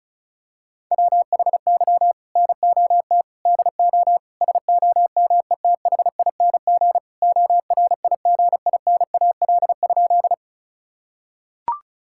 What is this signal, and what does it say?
Morse code, heard as a keyed tone, about 34 wpm, 700 Hz. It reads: WHY NOT DO SOMETHING ORIGINAL?
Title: Sentences from Top 1000 Words - Review Part 6 - 35wpm